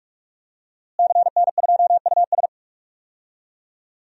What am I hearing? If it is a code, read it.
Morse code, heard as a keyed tone, about 48 wpm, 700 Hz. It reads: CN2US
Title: DXCC-100 Call Signs - 5 Characters - Repeated 2x - 45wpm